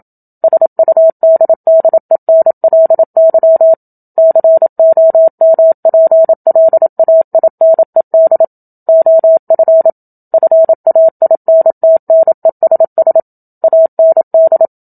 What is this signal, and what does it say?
Morse code, heard as a keyed tone, about 27 wpm, 650 Hz. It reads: SUDDENLY COMPLAINED OF FAINTNESS AND
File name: Book Two / 01-Chapter 1 - 27 WPM